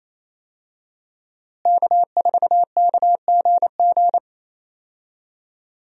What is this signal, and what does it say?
Morse code, heard as a keyed tone, about 28 wpm, 700 Hz. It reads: K4KGG
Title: USA Call Signs - 5 Characters - Repeated 2x - 28wpm